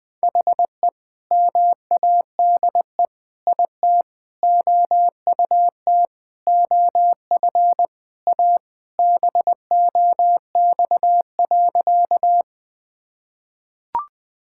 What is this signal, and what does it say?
Morse code, heard as a keyed tone, about 20 wpm, 700 Hz. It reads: HE MADE IT OUT OF A BOX.
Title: Sentences with 3-Word Phrases - Spoken Beginning - 20wpm